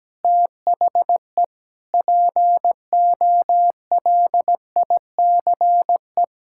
Morse code, heard as a keyed tone, about 17 wpm, 700 Hz. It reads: THE POLICE
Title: Pangram Challenge v2 - 17wpm